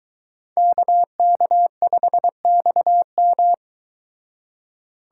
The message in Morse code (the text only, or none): KK5XM